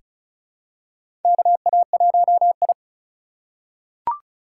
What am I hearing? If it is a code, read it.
Morse code, heard as a keyed tone, about 34 wpm, 700 Hz. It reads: KA1I
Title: USA Call Signs - 4 Characters - 4x Character Spacing - 35wpm